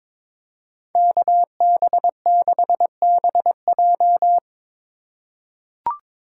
Morse code, heard as a keyed tone, about 23 wpm, 700 Hz. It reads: KB6BJ
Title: USA Call Signs - 5 Characters - 4x Character Spacing - 22wpm